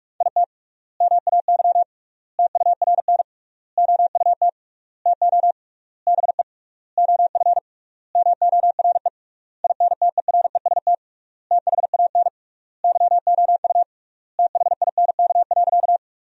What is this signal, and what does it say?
Morse code, heard as a keyed tone, about 46 wpm, 700 Hz. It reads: IT MAY TURN OUT TO BE OF MORE INTEREST THAN YOU THINK.
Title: 3x Word-Spacing - Sentences from Top 400 Words - 45wpm